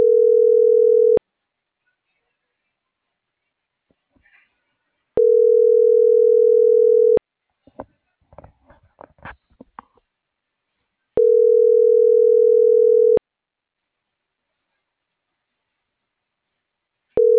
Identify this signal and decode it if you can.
Ringback tone (North America), 440 + 480 Hz.